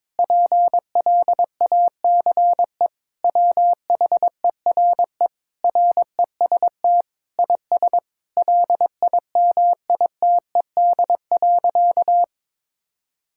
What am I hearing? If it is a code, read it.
Morse code, heard as a keyed tone, about 22 wpm, 700 Hz. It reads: PLACE WHERE REST IS LIMITED.
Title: Sentences from Top 1000 Words - Review Part 6 - 22wpm